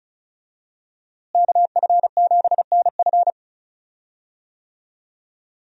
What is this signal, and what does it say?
Morse code, heard as a keyed tone, about 36 wpm, 700 Hz. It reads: KF7NF
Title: USA Call Signs - 5 Characters - Repeated 3x - 35wpm